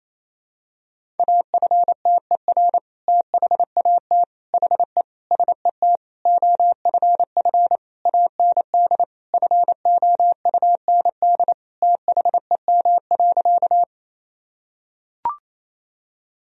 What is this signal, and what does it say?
Morse code, heard as a keyed tone, about 28 wpm, 700 Hz. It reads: AFTER THAT HE SET OFF AND FOUND THEM.
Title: Sentences from Top 1000 Words - Review Part 1 - 28wpm